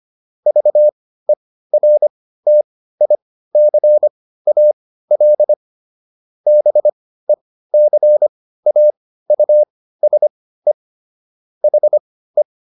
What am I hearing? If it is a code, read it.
Morse code, heard as a keyed tone, about 25 wpm, spaced as for 15 wpm, 600 Hz. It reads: VERTICAL BECAUSE HE